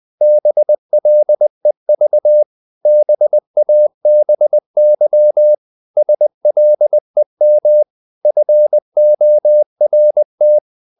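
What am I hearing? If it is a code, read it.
Morse code, heard as a keyed tone, about 20 wpm, 600 Hz. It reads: BLEV BABY SLEM FORT
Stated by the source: Morse code practice of short Danish words at 20 wpm, from CW med Gnister.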